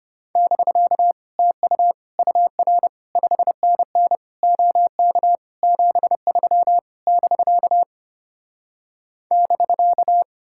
Morse code, heard as a keyed tone, about 33 wpm, 700 Hz. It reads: <BK> TU UR 5NN OK 73 <BK> <BK>